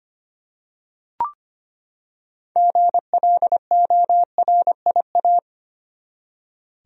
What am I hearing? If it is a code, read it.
Morse code, heard as a keyed tone, about 25 wpm, 700 Hz. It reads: GLORIA
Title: USA Common Names - 25wpm